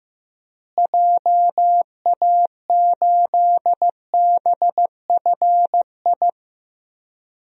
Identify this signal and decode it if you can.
Morse code, heard as a keyed tone, about 15 wpm, 700 Hz. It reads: JA8BFI
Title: DXCC-100 Call Signs - 6 Characters - Repeated 3x - 15wpm